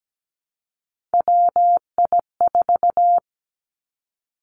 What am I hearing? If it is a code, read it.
Morse code, heard as a keyed tone, about 17 wpm, 700 Hz. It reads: WI4
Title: Three Challenging Letter-Number Combinations 17wpm